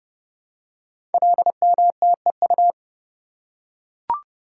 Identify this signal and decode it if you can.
Morse code, heard as a keyed tone, about 31 wpm, 700 Hz. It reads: LMTEU
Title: Code Groups - 5 Letters - 30wpm